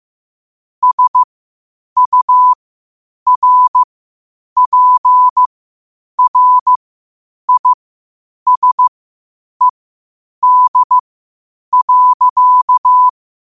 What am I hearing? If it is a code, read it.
Morse code, heard as a keyed tone, about 15 wpm, spaced as for 8 wpm, 1000 Hz. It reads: SURPRISED.
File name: qotdfiles / 8wpm